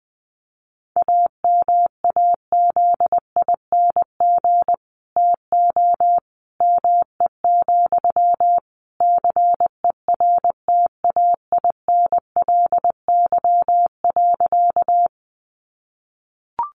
Morse code, heard as a keyed tone, about 20 wpm, 700 Hz. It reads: AMAZING TO ME, CERTAINLY.
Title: Sentences from Top 1500 Words 20wpm